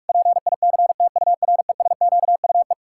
Morse code, heard as a keyed tone, about 46 wpm, 700 Hz. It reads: PICTURESQUE
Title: Pangram Challenge v2 - 45wpm